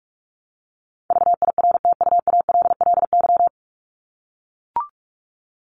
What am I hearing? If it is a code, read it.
Morse code, heard as a keyed tone, about 44 wpm, 700 Hz. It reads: VIRTUALLY
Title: Top 1501-1750 Words 45wpm